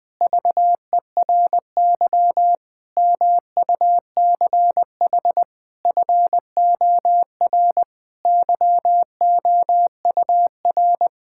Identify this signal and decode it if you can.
Morse code, heard as a keyed tone, about 20 wpm, 700 Hz. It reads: VERY MUCH FOR YOUR
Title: Sentences with 4-Word Phrases - Spoken Beginning - 20wpm